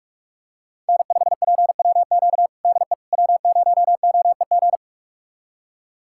Morse code, heard as a keyed tone, about 46 wpm, 700 Hz. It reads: N5PWQ DE W0OEG
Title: USA Callsign DE Callsign - 5 Characters - 45wpm